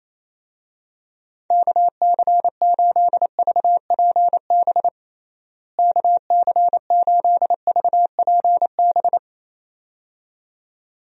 Morse code, heard as a keyed tone, about 28 wpm, 700 Hz. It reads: KC8VPB KC8VPB
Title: USA Call Signs - 6 Characters - Repeated 2x - 28wpm